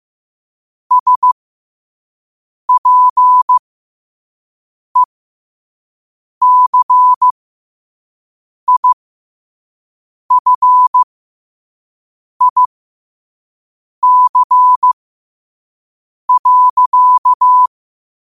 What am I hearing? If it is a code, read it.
Morse code, heard as a keyed tone, about 15 wpm, spaced as for 5 wpm, 1000 Hz. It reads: SPECIFIC.
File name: qotdfiles / 5wpm